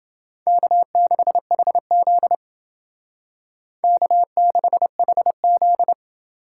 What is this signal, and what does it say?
Morse code, heard as a keyed tone, about 28 wpm, 700 Hz. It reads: K6HZ K6HZ